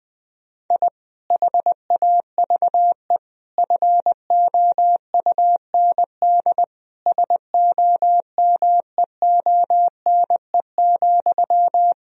Morse code, heard as a keyed tone, about 20 wpm, 700 Hz. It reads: I HAVE FOUND SOMEONE,
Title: Sentences from Top 1000 Words - Review Part 1 - 20wpm